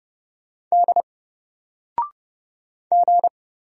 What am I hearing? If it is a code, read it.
Morse code, heard as a keyed tone, about 30 wpm, 700 Hz. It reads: D G
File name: Single-Letter-Number-30wpm